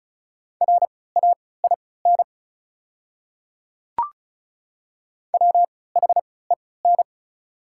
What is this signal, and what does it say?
Morse code, heard as a keyed tone, about 36 wpm, spaced as for 20 wpm, 700 Hz. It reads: RAIN WHEN
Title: 3x Character Spacing - Top 300 Words - 35wpm